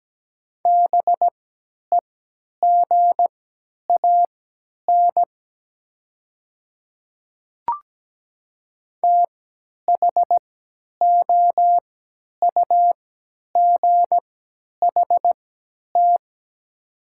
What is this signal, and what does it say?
Morse code, heard as a keyed tone, about 17 wpm, spaced as for 10 wpm, 700 Hz. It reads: BEGAN THOUGHT
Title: 3x Character Spacing - Top 300 Words - 17wpm